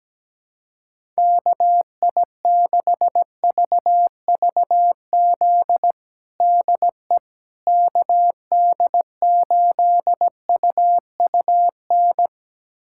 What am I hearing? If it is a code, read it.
Morse code, heard as a keyed tone, about 17 wpm, 700 Hz. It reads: KI6VVZ DE KD8UUN
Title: USA Callsign DE Callsign - 6 Characters - 17wpm